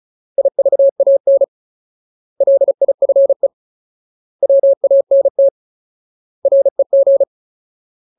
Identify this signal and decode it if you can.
Morse code, heard as a keyed tone, about 34 wpm, 550 Hz. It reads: IVAN LIFE WANT REG
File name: Ultimate Word List / 35WPM Ultimate